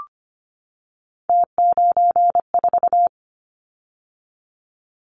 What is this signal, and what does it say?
Morse code, heard as a keyed tone, about 25 wpm, 700 Hz. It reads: T94
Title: Callsign Prefixes for Top 50 DXCC - 25wpm